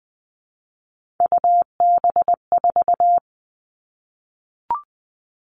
Morse code, heard as a keyed tone, about 20 wpm, 700 Hz. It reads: UB4
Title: All Cut Numbers - 3-Digit 20wpm